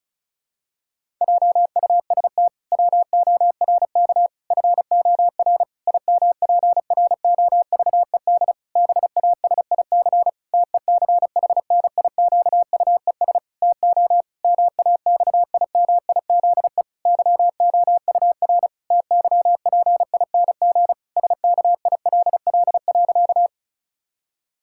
Morse code, heard as a keyed tone, about 36 wpm, 700 Hz. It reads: JUST WORK FOR IMPROVED BASIC TECHNIQUES TO MAXIMIZE YOUR TYPING SKILL.
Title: Pangram Challenge v2 - 35wpm